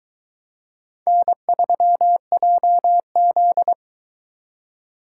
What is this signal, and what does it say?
Morse code, heard as a keyed tone, about 22 wpm, 700 Hz. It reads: N3JZ